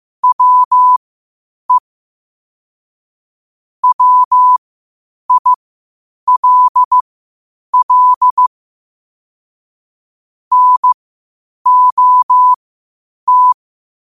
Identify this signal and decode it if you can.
Morse code, heard as a keyed tone, about 15 wpm, spaced as for 8 wpm, 1000 Hz. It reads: WE WILL NOT
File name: qotdfiles / 8wpm